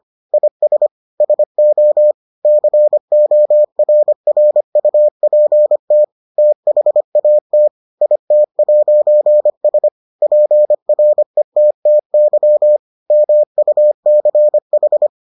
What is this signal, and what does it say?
Morse code, heard as a keyed tone, about 25 wpm, 600 Hz. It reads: IS SO CORRUPT THAT IT'S PRETTY MUCH